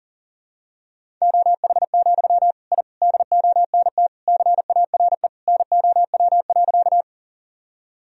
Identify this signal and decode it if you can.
Morse code, heard as a keyed tone, about 39 wpm, 700 Hz. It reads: OH, I DONT CARE NOW.
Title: Sentences from Top 1000 Words - Review Part 1 - 40wpm